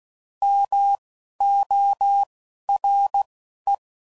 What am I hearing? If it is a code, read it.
Morse code, heard as a keyed tone, about 16 wpm, spaced as for 11 wpm, 800 Hz. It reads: MORE